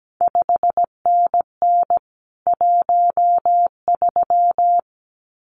Morse code, heard as a keyed tone, about 17 wpm, 700 Hz. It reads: 5NN 13
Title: CQ WW DX Contest Exchange Encore - 17wpm